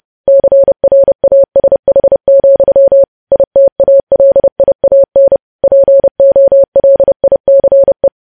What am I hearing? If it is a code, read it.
Morse code, heard as a keyed tone, about 30 wpm, 550 Hz. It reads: CRASH, ITALIAN POLICE
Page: News Headlines in Morse Code at 30 WPM